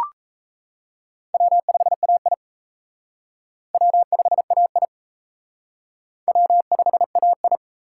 Morse code, heard as a keyed tone, about 34 wpm, 700 Hz. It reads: W5AI W5AI W5AI
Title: Speed Racing - USA 1x2 Call Signs 28wpm